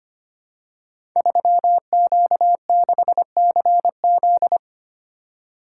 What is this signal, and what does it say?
Morse code, heard as a keyed tone, about 25 wpm, 700 Hz. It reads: 3Q6CZ
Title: Code Groups - 5 Letter:Numbers - 25wpm